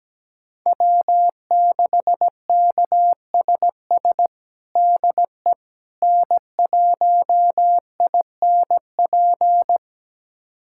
Morse code, heard as a keyed tone, about 17 wpm, 700 Hz. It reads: W6KSS DE N1INP
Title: USA Callsign DE Callsign - 5 Characters - 17wpm